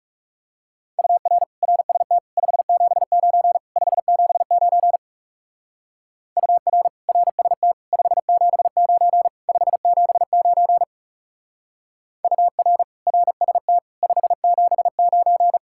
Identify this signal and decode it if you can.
Morse code, heard as a keyed tone, about 36 wpm, 700 Hz. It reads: UR RST 579 579 UR RST 579 579 UR RST 579
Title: Speed-Racing Common QSO Phrases Encore - 30wpm